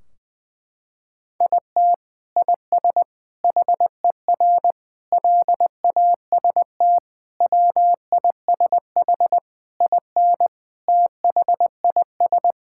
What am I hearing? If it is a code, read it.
Morse code, heard as a keyed tone, about 20 wpm, 700 Hz. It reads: IT IS HER LAST WISH IN THIS
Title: Sentences from Top 1600 Words 20wpm